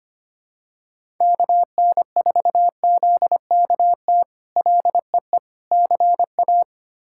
Morse code, heard as a keyed tone, about 25 wpm, 700 Hz. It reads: KN4ZKT LEE CA
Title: K1USN SST QSO Practice - 25wpm